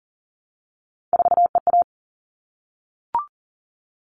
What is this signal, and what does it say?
Morse code, heard as a keyed tone, about 40 wpm, 700 Hz. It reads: 4EA
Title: All Cut Numbers - 3-Digit 40wpm